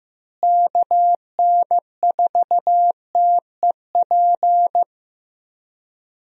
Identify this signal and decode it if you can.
Morse code, heard as a keyed tone, about 16 wpm, 700 Hz. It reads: KN4TEP
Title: USA Call Signs - 6 Characters - Repeated 3x - 15wpm